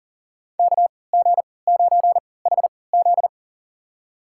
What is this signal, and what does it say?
Morse code, heard as a keyed tone, about 40 wpm, spaced as for 23 wpm, 700 Hz. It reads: KG9HZ
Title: USA Call Signs - 5 Characters - 3x Character Spacing - 40wpm